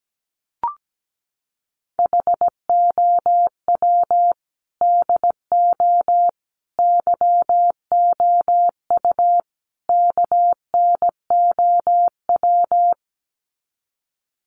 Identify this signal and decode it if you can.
Morse code, heard as a keyed tone, about 17 wpm, 700 Hz. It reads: HOW DO YOU KNOW
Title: Sets of 4 Words 17wpm